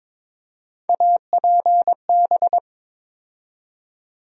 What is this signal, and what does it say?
Morse code, heard as a keyed tone, about 22 wpm, 700 Hz. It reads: APB
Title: USA Call Sign Postfixes - 3 Letters 22wpm